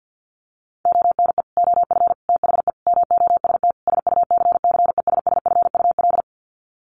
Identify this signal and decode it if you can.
Morse code, heard as a keyed tone, about 48 wpm, 700 Hz. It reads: ONE OF THE MOST SUCCESSFUL
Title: Sets of 5 Words Encore2 50wpm